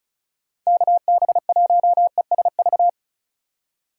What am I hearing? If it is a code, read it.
Morse code, heard as a keyed tone, about 43 wpm, 700 Hz. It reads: KB1ESV
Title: Speed Racing - USA 2x3 Call Signs 35wpm